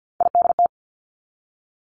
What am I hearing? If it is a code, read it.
Morse code, heard as a keyed tone, about 50 wpm, 700 Hz. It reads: IDT
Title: Three Challenging Letter-Number Combinations 50wpm